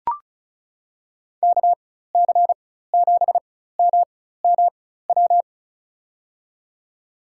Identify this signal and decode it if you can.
Morse code, heard as a keyed tone, about 36 wpm, spaced as for 16 wpm, 700 Hz. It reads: KC7MMW
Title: USA Call Signs - 6 Characters - 4x Character Spacing - 35wpm